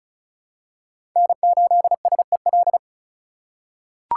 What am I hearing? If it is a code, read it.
Morse code, heard as a keyed tone, about 36 wpm, 700 Hz. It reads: N8SEL
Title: USA Call Signs - 5 Characters - 4x Character Spacing - 35wpm